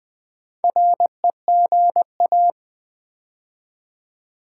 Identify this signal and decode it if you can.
Morse code, heard as a keyed tone, about 20 wpm, 690 Hz. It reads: REGA
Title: Build A Word - 2 Letters At A Time - 6-Letter Words - 20wpm